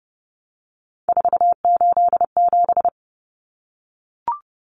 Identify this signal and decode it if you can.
Morse code, heard as a keyed tone, about 30 wpm, 700 Hz. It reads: 487
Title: Numbers - 3-Digit 30wpm